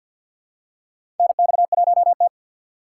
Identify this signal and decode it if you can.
Morse code, heard as a keyed tone, about 48 wpm, 700 Hz. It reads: NX1T
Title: USA Call Signs - 4 Characters - 6x Character Spacing - 50wpm